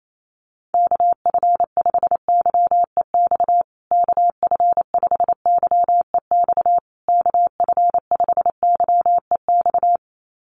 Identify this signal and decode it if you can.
Morse code, heard as a keyed tone, about 28 wpm, 700 Hz. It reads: KF5YEX KF5YEX KF5YEX
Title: USA Call Signs Repeated 3x 28wpm